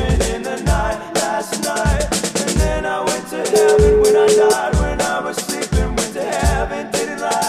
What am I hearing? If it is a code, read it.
Ringback tone (Europe, 425 Hz), with noise.